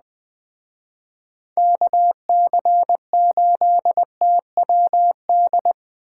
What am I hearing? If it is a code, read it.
Morse code, heard as a keyed tone, about 20 wpm, 700 Hz. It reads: KC8TWD
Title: Speed Racing - USA 2x3 Call Signs 17wpm